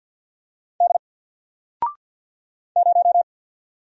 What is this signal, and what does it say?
Morse code, heard as a keyed tone, about 48 wpm, 700 Hz. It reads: D 0
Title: Single Letter-Number Alt 50wpm